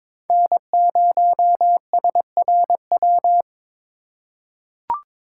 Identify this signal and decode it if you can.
Morse code, heard as a keyed tone, about 22 wpm, 700 Hz. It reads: N0SRW
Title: USA Call Signs - 5 Characters - 4x Character Spacing - 22wpm